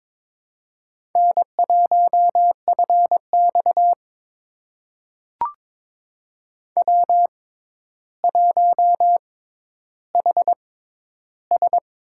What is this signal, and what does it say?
Morse code, heard as a keyed tone, about 22 wpm, spaced as for 11 wpm, 700 Hz. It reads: N1FX W1HS
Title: USA Call Signs - 4 Characters - 6x Character Spacing - 22wpm